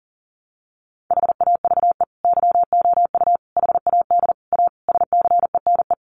Morse code, heard as a keyed tone, about 40 wpm, 700 Hz. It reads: HAVE YOU HAD A SCENE
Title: Sentences from Top 1100 Words 40wpm